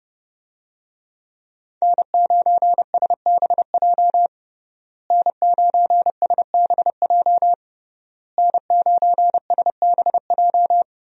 Morse code, heard as a keyed tone, about 30 wpm, 700 Hz. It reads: N9SBJ N9SBJ N9SBJ
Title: USA Call Signs - 5 Characters - Repeated 3x - 30wpm